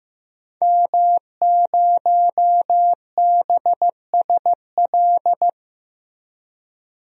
Morse code, heard as a keyed tone, about 15 wpm, 700 Hz. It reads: M0BSL
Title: DXCC-100 Call Signs - 5 Characters - Repeated 2x - 15wpm